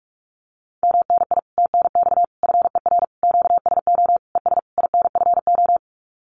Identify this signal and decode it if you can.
Morse code, heard as a keyed tone, about 44 wpm, 700 Hz. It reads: MNI TNX FER QSO ES INFO
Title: Common QSO Phrases Encore 45wpm